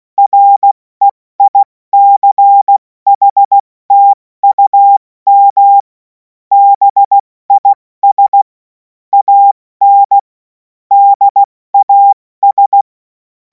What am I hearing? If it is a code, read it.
Morse code, heard as a keyed tone, about 16 wpm, 800 Hz. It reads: REICHTUM BIS AN DAS
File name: Morsekurs / Witze / Witz 30. Juni 2026 70 F80